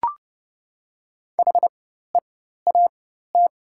Morse code, heard as a keyed tone, about 30 wpm, spaced as for 14 wpm, 700 Hz. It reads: HEAT
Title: 4x Character Spacing - Top 500 Words - 30wpm